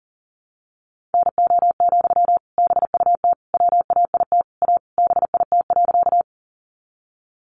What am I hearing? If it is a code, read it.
Morse code, heard as a keyed tone, about 40 wpm, 700 Hz. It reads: NO, BUT WAIT A BIT.
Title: Sentences from Top 900 Words 40wpm